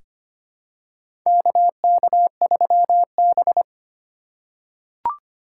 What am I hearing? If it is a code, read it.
Morse code, heard as a keyed tone, about 25 wpm, 700 Hz. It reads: KK3B
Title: USA Call Signs - 4 Characters - 4x Character Spacing - 25wpm